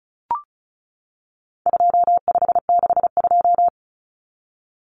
Morse code, heard as a keyed tone, about 34 wpm, 700 Hz. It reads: 2562
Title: Common Cut Numbers - 4-Digit 35wpm